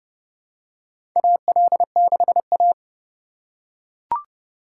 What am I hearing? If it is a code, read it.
Morse code, heard as a keyed tone, about 32 wpm, 700 Hz. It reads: AL6A
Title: USA Call Signs - 4 Characters - 2x Character Spacing - 30wpm